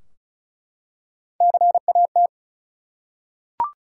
Morse code, heard as a keyed tone, about 36 wpm, 700 Hz. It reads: CAT
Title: Three Letter Words - 35wpm